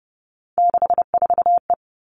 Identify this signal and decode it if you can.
Morse code, heard as a keyed tone, about 30 wpm, 700 Hz. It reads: 64E
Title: All Cut Numbers - 3-Digit 30wpm